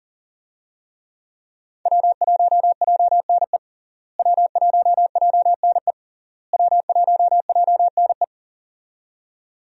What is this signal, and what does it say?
Morse code, heard as a keyed tone, about 39 wpm, 700 Hz. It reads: W1JNE W1JNE W1JNE
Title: USA Call Signs - 5 Characters - Repeated 3x - 40wpm